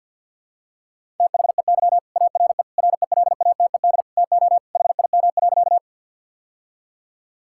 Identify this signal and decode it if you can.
Morse code, heard as a keyed tone, about 52 wpm, 700 Hz. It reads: THEY ARE RELATED TO HIM.
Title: Sentences from Top 1000 Words - Review Part 5 - 50wpm